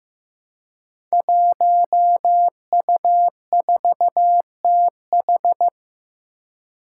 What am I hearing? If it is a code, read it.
Morse code, heard as a keyed tone, about 15 wpm, 700 Hz. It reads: 1U4TH
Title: Five Challenging Letter-Number Combinations 15wpm